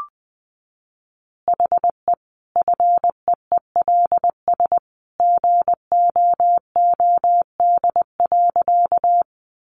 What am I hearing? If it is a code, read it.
Morse code, heard as a keyed tone, about 20 wpm, 700 Hz. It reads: HE FEELS GOOD.
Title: Additional Easy Phrases 20wpm